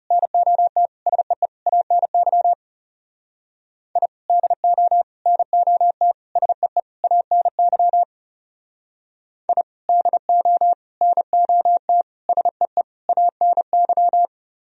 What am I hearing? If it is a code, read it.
Morse code, heard as a keyed tone, about 39 wpm, 700 Hz. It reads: NOT SEE ANY I DO NOT SEE ANY I DO NOT SEE ANY